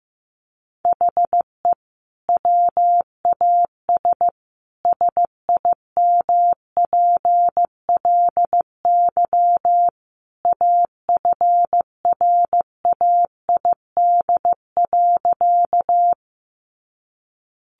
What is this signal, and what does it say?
Morse code, heard as a keyed tone, about 14 wpm, 700 Hz. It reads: HE WAS SIMPLY AFRAID.
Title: Sentences from Top 1500 Words 15wpm